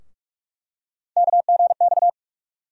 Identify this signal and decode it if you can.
Morse code, heard as a keyed tone, about 44 wpm, 700 Hz. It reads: KGX
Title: USA Call Sign Postfixes - 3 Letters 45wpm